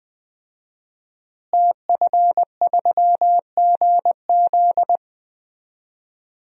Morse code, heard as a keyed tone, about 20 wpm, 700 Hz. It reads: TF3GZ
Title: DXCC-100 Call Signs - 5 Characters - Repeated 3x - 20wpm